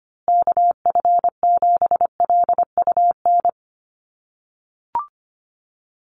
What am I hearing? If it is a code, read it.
Morse code, heard as a keyed tone, about 25 wpm, 700 Hz. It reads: KF7LUN
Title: USA Call Signs Repeated 3x 25wpm